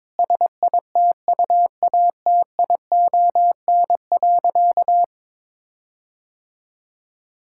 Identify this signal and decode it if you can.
Morse code, heard as a keyed tone, about 22 wpm, 700 Hz. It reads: SITUATION.
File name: practice / 22wpm / Sentences with 8-Word Phrases - 22wpm